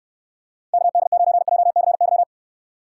Morse code, heard as a keyed tone, about 66 wpm, 700 Hz. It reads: KD9JCY